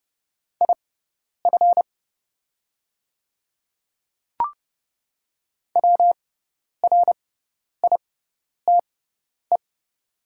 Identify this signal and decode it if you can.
Morse code, heard as a keyed tone, about 30 wpm, spaced as for 10 wpm, 700 Hz. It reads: IF WRITE
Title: 6x Character Spacing - Top 200 Words - 30wpm